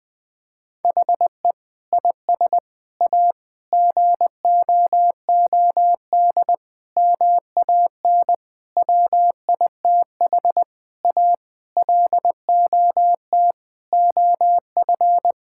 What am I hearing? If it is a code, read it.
Morse code, heard as a keyed tone, about 21 wpm, 700 Hz. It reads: HE IS A GOOD MAN WITH A LOT OF
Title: Sentences from Top 1000 Words - Review Part 6 - 20wpm